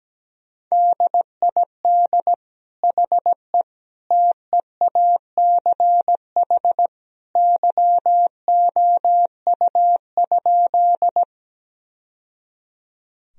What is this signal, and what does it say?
Morse code, heard as a keyed tone, about 17 wpm, 700 Hz. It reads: DID HE TEACH YOU?
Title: Sentences from Top 1000 Words - Review Part 2 - 17wpm